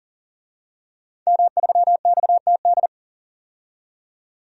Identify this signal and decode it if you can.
Morse code, heard as a keyed tone, about 40 wpm, 700 Hz. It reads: M3XTD
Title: DXCC-100 Call Signs - 5 Characters - 40wpm